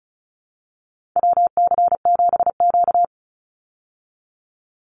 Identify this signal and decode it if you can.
Morse code, heard as a keyed tone, about 34 wpm, 700 Hz. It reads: WC7Q
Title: Call Signs Repeated 2x - 35wpm